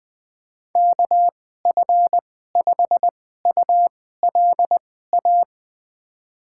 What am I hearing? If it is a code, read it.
Morse code, heard as a keyed tone, about 21 wpm, spaced as for 15 wpm, 700 Hz. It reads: KF5ULA
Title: USA Call Signs - 6 Characters - 2x Character Spacing - 20wpm